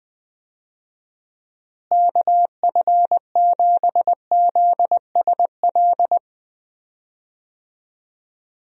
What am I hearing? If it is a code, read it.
Morse code, heard as a keyed tone, about 20 wpm, 700 Hz. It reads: KF7ZSL
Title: USA Call Signs - 6 Characters - Repeated 2x - 20wpm